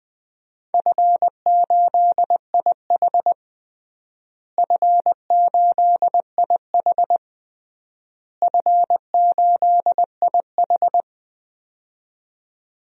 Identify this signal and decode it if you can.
Morse code, heard as a keyed tone, about 20 wpm, 700 Hz. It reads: F8IH F8IH F8IH
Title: DXCC-100 Call Signs - 4 Characters - Repeated 3x - 20wpm